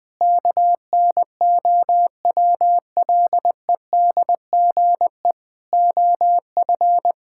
Morse code, heard as a keyed tone, about 20 wpm, 700 Hz. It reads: KNOWLEDGE OF
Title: Sets of 9 Words - 20wpm